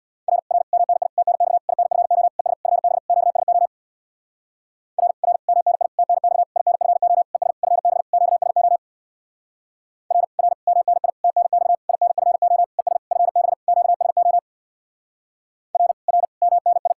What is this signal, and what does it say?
Morse code, heard as a keyed tone, about 66 wpm, 700 Hz. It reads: R R MNI TNX INFO ES FB QSO R R MNI TNX INFO ES FB QSO R R MNI TNX INFO ES FB QSO R R MNI